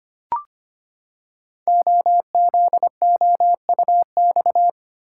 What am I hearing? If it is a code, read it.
Morse code, heard as a keyed tone, about 25 wpm, 700 Hz. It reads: OZOUX
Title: Code Groups - 5 Letter:Numbers - 25wpm